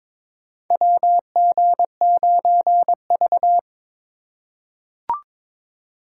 Morse code, heard as a keyed tone, about 22 wpm, 700 Hz. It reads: WG9V
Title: USA Call Signs - 4 Characters - 4x Character Spacing - 22wpm